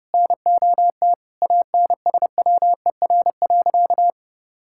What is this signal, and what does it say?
Morse code, heard as a keyed tone, about 32 wpm, 700 Hz. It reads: NOT ANSWER.
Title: Sentences from Top 1000 Words - Review Part 6 - 30wpm